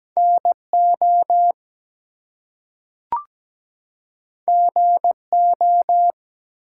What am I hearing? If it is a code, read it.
Morse code, heard as a keyed tone, about 17 wpm, 700 Hz. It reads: NO GO
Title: Two Letter Words - 17wpm